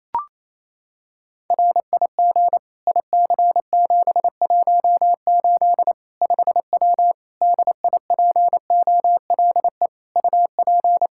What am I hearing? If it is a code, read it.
Morse code, heard as a keyed tone, about 28 wpm, 700 Hz. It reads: RIG IC718 5W DIPOLE UP
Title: K1USN SST QSO Practice 28wpm